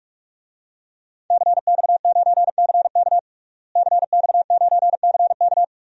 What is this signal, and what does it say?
Morse code, heard as a keyed tone, about 46 wpm, 700 Hz. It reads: CX9CK CX9CK
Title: DXCC-100 Call Signs - 5 Characters - Repeated 2x - 45wpm